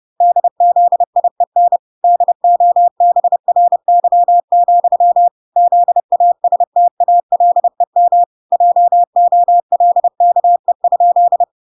Morse code, heard as a keyed tone, about 30 wpm, 700 Hz. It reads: DZIEN DOBRY, ZASTALEM JOLKE?